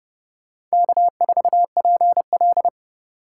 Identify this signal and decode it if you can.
Morse code, heard as a keyed tone, about 36 wpm, 700 Hz. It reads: K4PL
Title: Speed Racing - USA Call Signs 30wpm